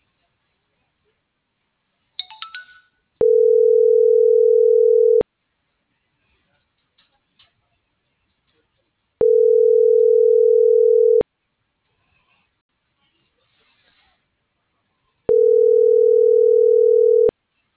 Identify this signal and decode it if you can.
Ringback tone (North America), 440 + 480 Hz.